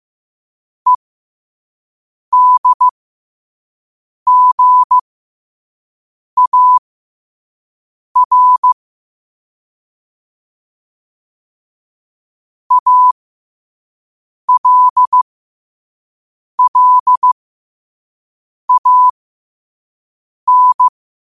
Morse code, heard as a keyed tone, about 15 wpm, spaced as for 5 wpm, 1000 Hz. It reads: EDGAR ALLAN
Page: New quotes every day in morse code at 5 Words per minute.